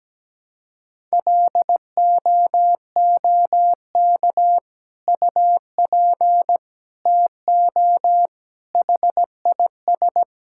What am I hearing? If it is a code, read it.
Morse code, heard as a keyed tone, about 17 wpm, 700 Hz. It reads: LOOK UP TO HIS
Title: Sentences from Top 1600 Words 17wpm